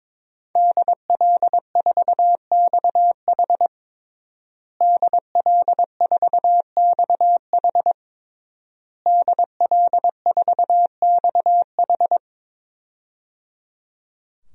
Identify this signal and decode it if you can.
Morse code, heard as a keyed tone, about 22 wpm, 700 Hz. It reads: DL4XH DL4XH DL4XH
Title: DXCC-100 Call Signs - 5 Characters - Repeated 3x - 22wpm